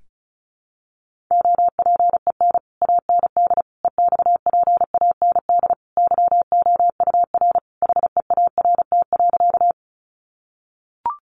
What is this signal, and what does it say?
Morse code, heard as a keyed tone, about 34 wpm, 700 Hz. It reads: OPEN AND EXPAND YOUR HEART.
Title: Sentences from Top 1400 Words 35wpm